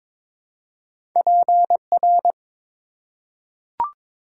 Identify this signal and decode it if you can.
Morse code, heard as a keyed tone, about 22 wpm, 700 Hz. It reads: PR
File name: ARRL Sections v2 - 22wpm